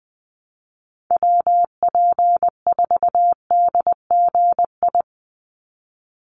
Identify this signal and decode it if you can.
Morse code, heard as a keyed tone, about 20 wpm, 700 Hz. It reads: WP4DGI
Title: USA Call Signs 20wpm